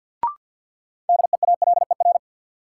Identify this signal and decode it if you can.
Morse code, heard as a keyed tone, about 51 wpm, 700 Hz. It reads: DEALER
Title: Sentences from Top 1100 Words - New Word Drill - 50wpm